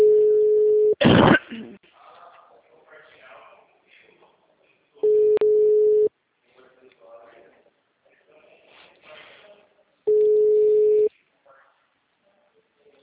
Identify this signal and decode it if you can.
Ringback tone (Europe, 425 Hz), with speech.